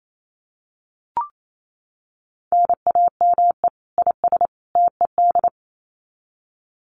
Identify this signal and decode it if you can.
Morse code, heard as a keyed tone, about 28 wpm, 700 Hz. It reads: NAME IS TED
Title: Common QSO Phrases 28wpm